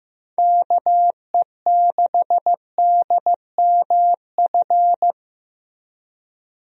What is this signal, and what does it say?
Morse code, heard as a keyed tone, about 16 wpm, 700 Hz. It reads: KE6DMF
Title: USA Call Signs - 6 Characters - Repeated 3x - 15wpm